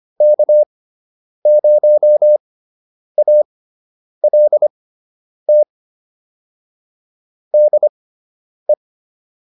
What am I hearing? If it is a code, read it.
Morse code, heard as a keyed tone, about 25 wpm, spaced as for 9 wpm, 600 Hz. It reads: K0ALT DE
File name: QSO 202 09